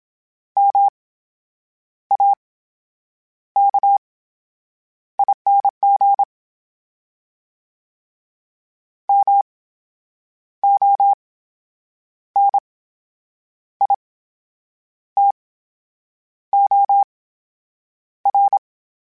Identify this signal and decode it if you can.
Morse code, heard as a keyed tone, about 27 wpm, spaced as for 8 wpm, 800 Hz. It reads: MAKING MONITOR
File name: Special INGs / Special INGs-mp3s / VerySlowToFaster Ing @25